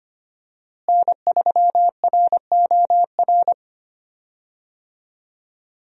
Morse code, heard as a keyed tone, about 25 wpm, 700 Hz. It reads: N3ROR
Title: USA Call Signs - 5 Characters - Repeated 2x - 25wpm